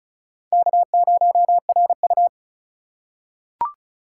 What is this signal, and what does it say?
Morse code, heard as a keyed tone, about 34 wpm, 700 Hz. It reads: K0RU
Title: USA Call Signs - 4 Characters - 3x Character Spacing - 35wpm